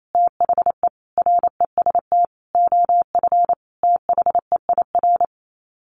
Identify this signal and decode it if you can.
Morse code, heard as a keyed tone, about 28 wpm, 700 Hz. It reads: THE REST OF THEIR
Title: Sets of 5 Words Encore2 28wpm